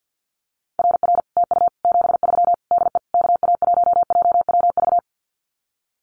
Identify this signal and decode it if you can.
Morse code, heard as a keyed tone, about 50 wpm, 700 Hz. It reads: RR TU 73 DE KA1JWV
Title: Common QSO Phrases Encore 50wpm